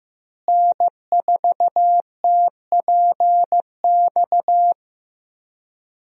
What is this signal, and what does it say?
Morse code, heard as a keyed tone, about 15 wpm, 700 Hz. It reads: N4TPX
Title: USA Call Signs - 5 Characters - Repeated 3x - 15wpm